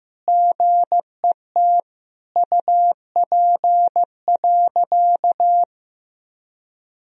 Morse code, heard as a keyed tone, about 15 wpm, 700 Hz. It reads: GET UP.
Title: Sentences from Top 500 Words and Abbreviations 15wpm